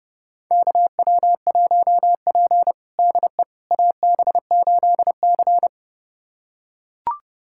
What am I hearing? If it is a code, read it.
Morse code, heard as a keyed tone, about 30 wpm, 700 Hz. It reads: KW1P DE AB8C
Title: USA Callsign DE Callsign - 4 Characters - 30wpm